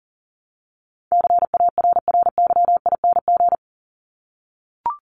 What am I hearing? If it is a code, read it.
Morse code, heard as a keyed tone, about 40 wpm, 700 Hz. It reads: CARRYING
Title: ING Suffix - Top 100 Words with Base Words 40wpm